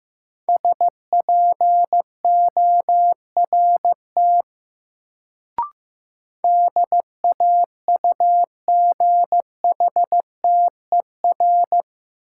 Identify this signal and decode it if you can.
Morse code, heard as a keyed tone, about 15 wpm, 700 Hz. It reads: SPORT DAUGHTER
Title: Sentences from Top 1500 Words - New Word Drill Alt - 15wpm